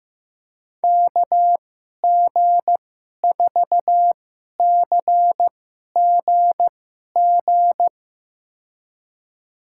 Morse code, heard as a keyed tone, about 15 wpm, spaced as for 11 wpm, 700 Hz. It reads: KG4CGG
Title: USA Call Signs - 6 Characters - 2x Character Spacing - 15wpm